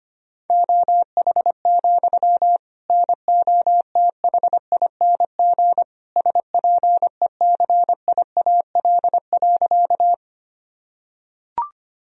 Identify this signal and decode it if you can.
Morse code, heard as a keyed tone, about 25 wpm, 700 Hz. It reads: OH, NOTHING SPECIAL.
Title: Sentences from Top 1000 Words - Review Part 3 - 25wpm